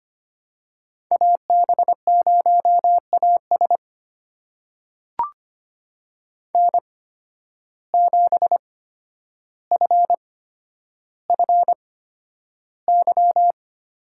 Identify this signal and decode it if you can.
Morse code, heard as a keyed tone, about 25 wpm, spaced as for 11 wpm, 700 Hz. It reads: AB0AS N7FFY
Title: USA Call Signs - 5 Characters - 8x Character Spacing - 25wpm